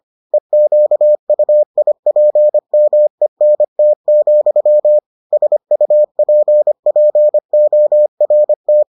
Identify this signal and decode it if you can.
Morse code, heard as a keyed tone, about 25 wpm, 600 Hz. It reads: EQUIPMENT, SUPPORT